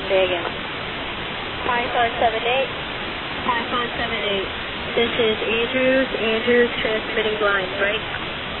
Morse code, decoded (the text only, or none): none